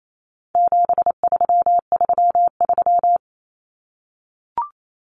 Morse code, heard as a keyed tone, about 28 wpm, 700 Hz. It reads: 7333
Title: Numbers - 4-Digit 28wpm